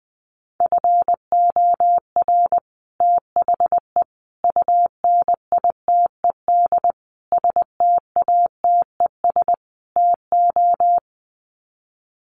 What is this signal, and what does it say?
Morse code, heard as a keyed tone, about 20 wpm, 700 Hz. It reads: FOR THE UNITED STATES TO
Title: Sets of 5 Words Encore 20wpm